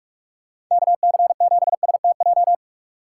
Morse code, heard as a keyed tone, about 47 wpm, 700 Hz. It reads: KC7STJ
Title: USA Call Signs - 6 Characters - 4x Character Spacing - 45wpm